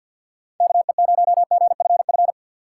Morse code, heard as a keyed tone, about 49 wpm, 700 Hz. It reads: KE0GFF
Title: USA Call Signs - 6 Characters - 6x Character Spacing - 50wpm